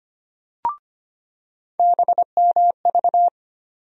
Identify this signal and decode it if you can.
Morse code, heard as a keyed tone, about 25 wpm, 700 Hz. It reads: BMV
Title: USA Call Sign Postfixes - 3 Letters 25wpm